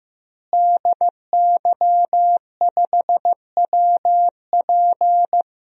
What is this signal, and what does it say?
Morse code, heard as a keyed tone, about 15 wpm, 700 Hz. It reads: DY5WP
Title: DXCC-100 Call Signs - 5 Characters - Repeated 3x - 15wpm